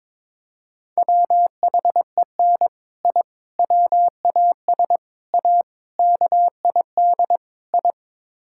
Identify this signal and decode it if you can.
Morse code, heard as a keyed tone, about 21 wpm, 700 Hz. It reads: WHEN I WAS A KID I
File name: practice / 22wpm / Sets of 6 Words - 22wpm